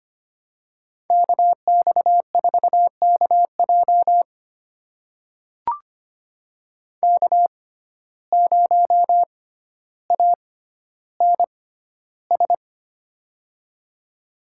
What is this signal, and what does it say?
Morse code, heard as a keyed tone, about 25 wpm, spaced as for 13 wpm, 700 Hz. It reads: KX4KJ K0ANS
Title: USA Call Signs - 5 Characters - 6x Character Spacing - 25wpm